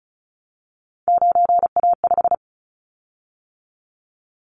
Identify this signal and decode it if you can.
Morse code, heard as a keyed tone, about 34 wpm, 700 Hz. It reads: 9A5
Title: Callsign Prefixes for Top 50 DXCC - 35wpm